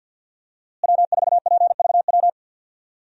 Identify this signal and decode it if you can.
Morse code, heard as a keyed tone, about 52 wpm, 700 Hz. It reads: W4PVW
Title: USA Call Signs - 5 Characters - Repeated 3x - 50wpm